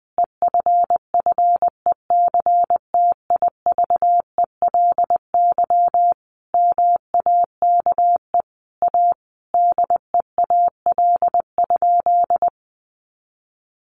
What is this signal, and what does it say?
Morse code, heard as a keyed tone, about 20 wpm, 700 Hz. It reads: EFFECTIVELY MAKE A DEAL?
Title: Sentences from Top 1400 Words 20wpm